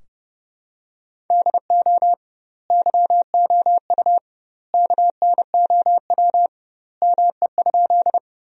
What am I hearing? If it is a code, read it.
Morse code, heard as a keyed tone, about 30 wpm, 700 Hz. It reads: DO YOU KNOW ME?
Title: 2x Word-Spacing - Sentences from Top 200 Words - 30wpm